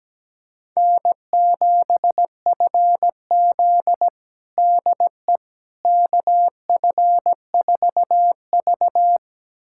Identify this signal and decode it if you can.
Morse code, heard as a keyed tone, about 17 wpm, 700 Hz. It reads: N7FZ DE KF4V
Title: USA Callsign DE Callsign - 4 Characters - 17wpm